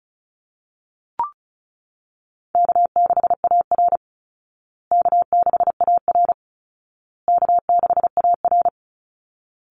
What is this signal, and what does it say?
Morse code, heard as a keyed tone, about 34 wpm, 700 Hz. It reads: K6AR K6AR K6AR
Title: Easy Call Signs Repeated 3x - 35wpm